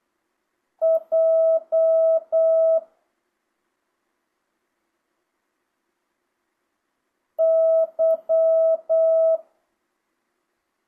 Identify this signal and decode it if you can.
Morse code, heard as a keyed tone, about 8 wpm, 650 Hz. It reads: J Y
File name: jh8cbh / 2025cw